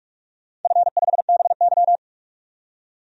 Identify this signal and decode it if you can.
Morse code, heard as a keyed tone, about 46 wpm, 700 Hz. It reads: F5BY
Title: DXCC-100 Call Signs - 4 Characters - 45wpm